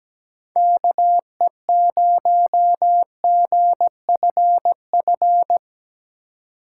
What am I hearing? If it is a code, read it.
Morse code, heard as a keyed tone, about 17 wpm, 700 Hz. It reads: KE0GFF
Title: USA Call Signs - 6 Characters - 2x Character Spacing - 17wpm